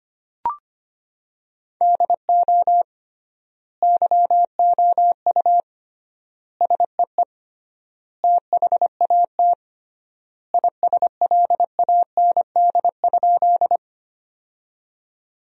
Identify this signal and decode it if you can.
Morse code, heard as a keyed tone, about 25 wpm, 700 Hz. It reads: DO YOU SEE THAT ISLAND?
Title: 3x Word-Spacing - Sentences from Top 500 Words - 25wpm